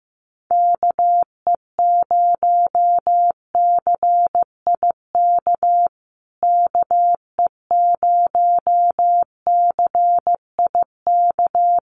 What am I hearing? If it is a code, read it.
Morse code, heard as a keyed tone, about 15 wpm, 700 Hz. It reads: KE0CIK KE0CIK
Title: USA Call Signs Repeated 3x 15wpm